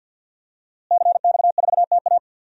Morse code, heard as a keyed tone, about 52 wpm, 700 Hz. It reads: CX4TA
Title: DXCC-100 Call Signs - 5 Characters - Repeated 3x - 50wpm